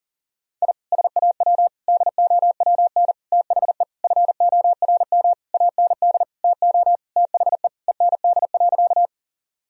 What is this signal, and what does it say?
Morse code, heard as a keyed tone, about 40 wpm, 700 Hz. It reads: I SAW DOWN THE FORM AND TO THE END.
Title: Sentences from Top 1000 Words - Review Part 1 - 40wpm